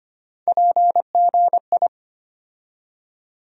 Morse code, heard as a keyed tone, about 25 wpm, 700 Hz. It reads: PGI
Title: USA Call Sign Postfixes - 3 Letters 25wpm